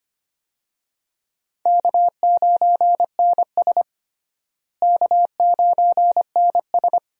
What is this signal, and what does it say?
Morse code, heard as a keyed tone, about 25 wpm, 700 Hz. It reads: K9NS K9NS
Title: USA Call Signs - 4 Characters - Repeated 3x - 25wpm